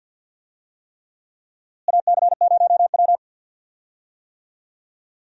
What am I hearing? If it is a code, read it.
Morse code, heard as a keyed tone, about 47 wpm, 700 Hz. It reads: AC0W
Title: USA Call Signs - 4 Characters - Repeated 3x - 50wpm